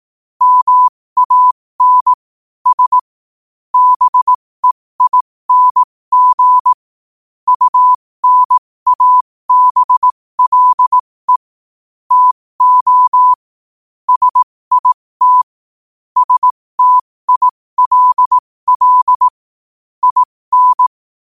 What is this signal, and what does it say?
Morse code, heard as a keyed tone, about 18 wpm, spaced as for 15 wpm, 1000 Hz. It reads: MAN S BEING UNABLE TO SIT STILL IN
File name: qotdfiles / 15wpm